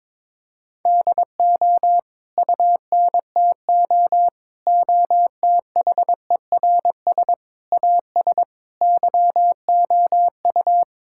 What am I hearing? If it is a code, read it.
Morse code, heard as a keyed tone, about 22 wpm, 700 Hz. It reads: DO UNTO OTHERS AS YOU
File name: practice / 22wpm / Sets of 8 Words - 22wpm